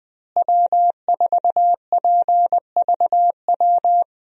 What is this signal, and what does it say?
Morse code, heard as a keyed tone, about 20 wpm, 700 Hz. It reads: W4PVW
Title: USA Call Signs - 5 Characters - Repeated 3x - 20wpm